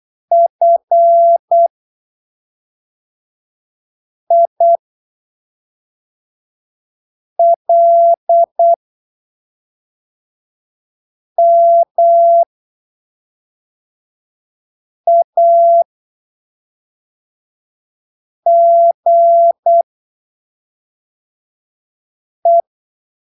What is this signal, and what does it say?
Morse code, heard as a keyed tone, about 8 wpm, spaced as for 3 wpm, 680 Hz. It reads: FILMAGE